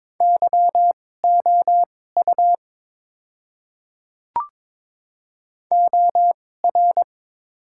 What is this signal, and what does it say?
Morse code, heard as a keyed tone, about 21 wpm, spaced as for 16 wpm, 690 Hz. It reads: YOU OR
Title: 2x Character Spacing - Top 100 Words - 22wpm